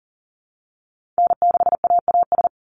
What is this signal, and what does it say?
Morse code, heard as a keyed tone, about 40 wpm, 700 Hz. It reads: N6AAS
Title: USA Call Signs 40wpm